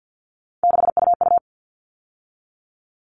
Morse code, heard as a keyed tone, about 53 wpm, 700 Hz. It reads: 6UU
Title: All Cut Numbers - 3-Digit 50wpm